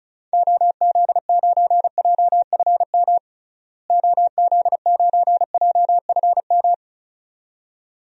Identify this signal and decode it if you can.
Morse code, heard as a keyed tone, about 36 wpm, 700 Hz. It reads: OZ9JFM OZ9JFM
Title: DXCC-100 Call Signs - 6 Characters - Repeated 2x - 35wpm